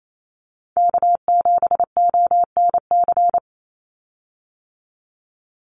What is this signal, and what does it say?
Morse code, heard as a keyed tone, about 28 wpm, 700 Hz. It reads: K7ONC
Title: USA Call Signs 28wpm